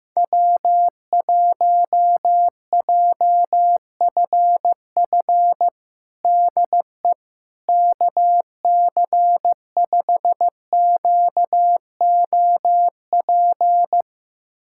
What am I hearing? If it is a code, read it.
Morse code, heard as a keyed tone, about 15 wpm, 700 Hz. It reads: W1JFF DE KC5QOP
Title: USA Callsign DE Callsign 15wpm